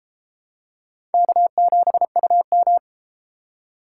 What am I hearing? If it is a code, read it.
Morse code, heard as a keyed tone, about 31 wpm, 700 Hz. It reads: K7UM